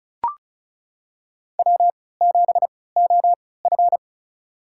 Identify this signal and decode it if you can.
Morse code, heard as a keyed tone, about 34 wpm, spaced as for 20 wpm, 700 Hz. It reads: W7OF
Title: USA Call Signs - 4 Characters - 3x Character Spacing - 35wpm